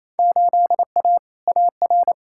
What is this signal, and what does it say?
Morse code, heard as a keyed tone, about 28 wpm, 700 Hz. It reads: 8A AR
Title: Field Day Exchanges Encore - 28wpm